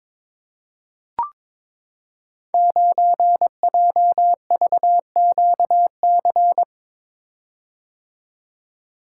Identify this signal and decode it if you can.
Morse code, heard as a keyed tone, about 21 wpm, 700 Hz. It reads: 9JVQC
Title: Code Groups - 5 Letter:Numbers - 22wpm